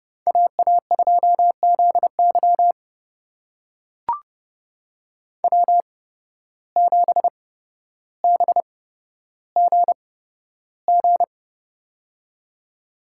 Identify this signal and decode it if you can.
Morse code, heard as a keyed tone, about 29 wpm, spaced as for 13 wpm, 700 Hz. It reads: AA2ZY W7BGG
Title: USA Call Signs - 5 Characters - 8x Character Spacing - 30wpm